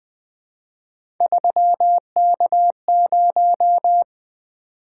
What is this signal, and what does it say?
Morse code, heard as a keyed tone, about 20 wpm, 700 Hz. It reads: 3K0
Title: Three Letter-Number Combinations - 20wpm